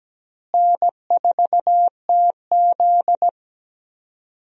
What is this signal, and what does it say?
Morse code, heard as a keyed tone, about 17 wpm, 700 Hz. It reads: N4TZ
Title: USA Call Signs - 4 Characters - Repeated 2x - 17wpm